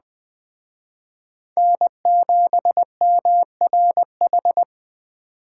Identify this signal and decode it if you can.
Morse code, heard as a keyed tone, about 20 wpm, 700 Hz. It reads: N7MRH
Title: Speed Racing - USA 1x3 Call Signs 17wpm